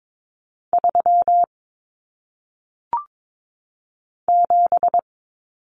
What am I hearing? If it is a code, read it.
Morse code, heard as a keyed tone, about 22 wpm, 700 Hz. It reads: 3 7
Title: Numbers - 1-Digit 22wpm